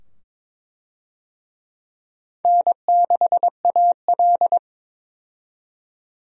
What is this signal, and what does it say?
Morse code, heard as a keyed tone, about 22 wpm, 700 Hz. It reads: N6AL
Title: USA Call Signs - 4 Characters - 22wpm